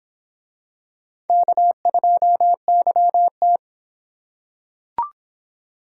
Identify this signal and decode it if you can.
Morse code, heard as a keyed tone, about 25 wpm, 700 Hz. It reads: K2YT